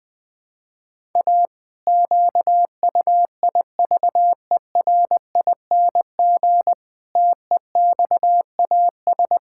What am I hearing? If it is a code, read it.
Morse code, heard as a keyed tone, about 20 wpm, 700 Hz. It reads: A QUIVERING TEXAS
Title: Pangram Challenge v2 - 20wpm